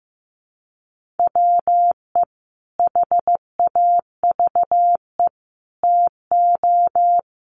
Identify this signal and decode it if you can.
Morse code, heard as a keyed tone, about 15 wpm, 700 Hz. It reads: WE HAVE TO
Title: Sets of 5 Words Encore2 15wpm